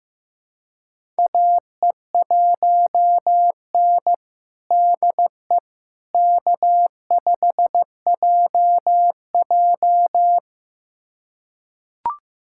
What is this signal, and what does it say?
Morse code, heard as a keyed tone, about 15 wpm, 700 Hz. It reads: AE1N DE K5JJ
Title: USA Callsign DE Callsign - 4 Characters - 15wpm